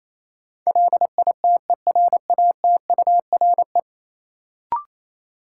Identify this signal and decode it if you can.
Morse code, heard as a keyed tone, about 28 wpm, 700 Hz. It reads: LITERATURE
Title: Sentences from Top 600 Words - New Word Drill - 28wpm